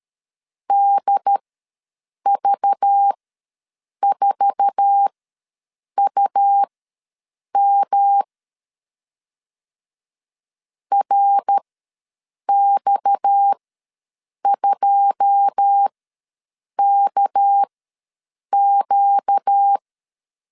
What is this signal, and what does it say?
Morse code, heard as a keyed tone, about 13 wpm, spaced as for 7 wpm, 800 Hz. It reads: DV4UM RX2KQ
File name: LEZIONI / 9 2 ws 12wpm 3406